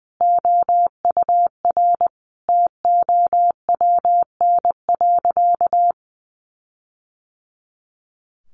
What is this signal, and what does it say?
Morse code, heard as a keyed tone, about 20 wpm, 700 Hz. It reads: OUR TOWN.
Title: Sentences from Top 1500 Words 20wpm